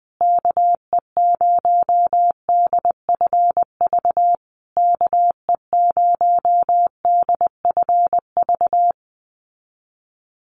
Morse code, heard as a keyed tone, about 20 wpm, 700 Hz. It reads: KE0DFV KE0DFV
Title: USA Call Signs Repeated 3x 20wpm